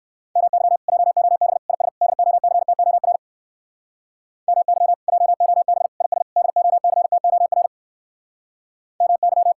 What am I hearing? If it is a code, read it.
Morse code, heard as a keyed tone, about 56 wpm, 700 Hz. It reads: MY JOB IS DOCTOR MY JOB IS DOCTOR MY
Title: Speed-Racing Common QSO Phrases - 45wpm